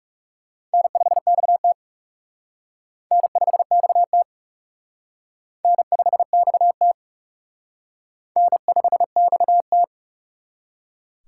Morse code, heard as a keyed tone, about 39 wpm, 700 Hz. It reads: N5XT N5XT N5XT N5XT